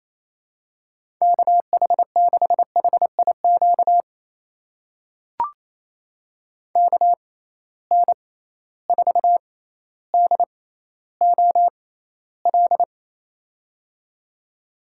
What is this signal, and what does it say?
Morse code, heard as a keyed tone, about 27 wpm, spaced as for 14 wpm, 700 Hz. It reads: KH6HIQ KN4DOL
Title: USA Call Signs - 6 Characters - 6x Character Spacing - 28wpm